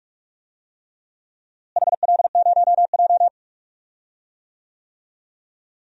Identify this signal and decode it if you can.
Morse code, heard as a keyed tone, about 45 wpm, 700 Hz. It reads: HL0J
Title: DXCC-100 Call Signs - 4 Characters - Repeated 3x - 45wpm